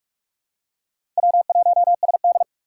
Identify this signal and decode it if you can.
Morse code, heard as a keyed tone, about 48 wpm, 700 Hz. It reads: W1SD
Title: USA Call Signs - 6x Character Spacing - 45wpm